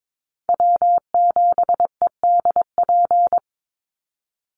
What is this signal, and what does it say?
Morse code, heard as a keyed tone, about 22 wpm, 700 Hz. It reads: W7EDP
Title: USA Call Signs 22wpm